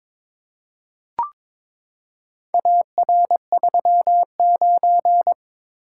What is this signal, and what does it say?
Morse code, heard as a keyed tone, about 21 wpm, 700 Hz. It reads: AR39
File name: practice / 22wpm / Four Letter-Number Combinations - 22wpm